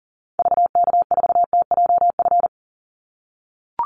Morse code, heard as a keyed tone, about 40 wpm, 700 Hz. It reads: VK4TJF
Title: Call Signs - 40wpm